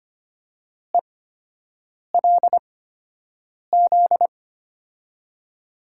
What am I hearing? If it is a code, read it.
Morse code, heard as a keyed tone, about 25 wpm, spaced as for 7 wpm, 700 Hz. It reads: ELZ
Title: USA Call Signs - 6 Characters - 8x Character Spacing - 25wpm